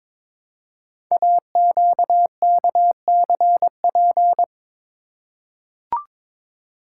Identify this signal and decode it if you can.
Morse code, heard as a keyed tone, about 22 wpm, 700 Hz. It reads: AQKCP
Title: Code Groups - 5 Letters - 22wpm